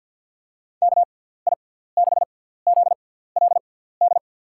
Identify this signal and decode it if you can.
Morse code, heard as a keyed tone, about 49 wpm, spaced as for 17 wpm, 700 Hz. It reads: KI6ZLD
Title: USA Call Signs - 6 Characters - 6x Character Spacing - 50wpm